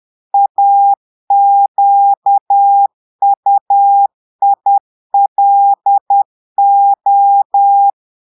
Morse code, heard as a keyed tone, about 10 wpm, 800 Hz. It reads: AQUILO
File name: portugais texte en clair 2 vitesse 10 mots minutes